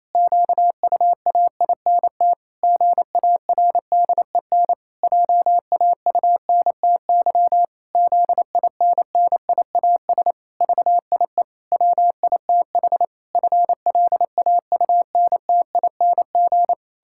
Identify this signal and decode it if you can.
Morse code, heard as a keyed tone, about 27 wpm, 700 Hz. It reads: QUAINT GARDEN JAUNTY ZINNIAS VIE WITH FLAUNTING
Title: Pangram Challenge v2 - 28wpm